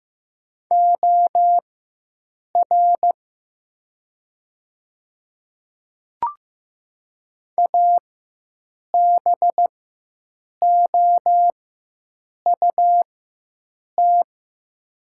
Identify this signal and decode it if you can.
Morse code, heard as a keyed tone, about 15 wpm, spaced as for 7 wpm, 700 Hz. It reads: OR ABOUT
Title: 4x Character Spacing - Top 100 Words - 15wpm